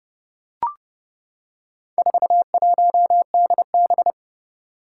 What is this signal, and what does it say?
Morse code, heard as a keyed tone, about 30 wpm, 700 Hz. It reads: 41DB
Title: Four Challenging Letter-Number Combinations 30wpm